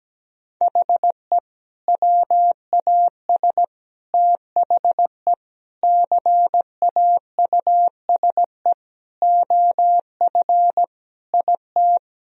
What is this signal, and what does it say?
Morse code, heard as a keyed tone, about 17 wpm, 700 Hz. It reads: HE WAS THE CAUSE OF IT
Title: Sentences from Top 1000 Words - Review Part 1 - 17wpm